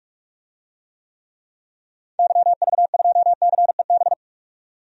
Morse code, heard as a keyed tone, about 46 wpm, 700 Hz. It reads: YV2CEB
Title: DXCC-100 Call Signs - 6 Characters - 45wpm